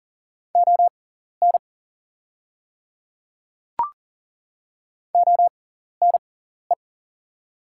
Morse code, heard as a keyed tone, about 40 wpm, spaced as for 14 wpm, 700 Hz. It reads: ON ONE
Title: 6x Character Spacing - Top 100 Words - 40wpm